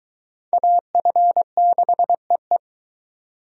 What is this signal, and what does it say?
Morse code, heard as a keyed tone, about 24 wpm, 700 Hz. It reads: AF6EE